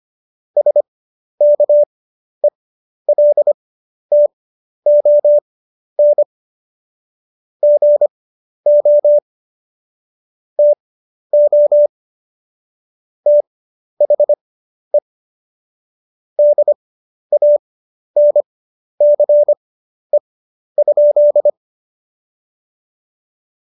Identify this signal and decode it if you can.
Morse code, heard as a keyed tone, about 25 wpm, spaced as for 11 wpm, 590 Hz. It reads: SKELTON GO TO THE DANCE?
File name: Jokes-25-char-10-eff